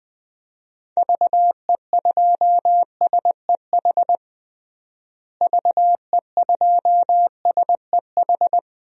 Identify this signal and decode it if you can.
Morse code, heard as a keyed tone, about 20 wpm, 700 Hz. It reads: VE2SEH VE2SEH
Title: DXCC-100 Call Signs - 6 Characters - Repeated 2x - 20wpm